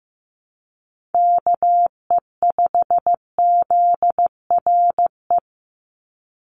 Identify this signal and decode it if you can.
Morse code, heard as a keyed tone, about 15 wpm, 700 Hz. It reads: KE5ZRE
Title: USA Call Signs 15wpm